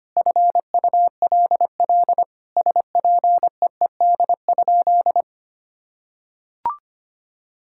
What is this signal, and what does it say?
Morse code, heard as a keyed tone, about 25 wpm, 700 Hz. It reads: FULL SPEED?
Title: Sentences from Top 1000 Words - Review Part 5 - 25wpm